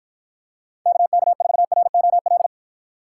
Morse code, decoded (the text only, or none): KK4ROL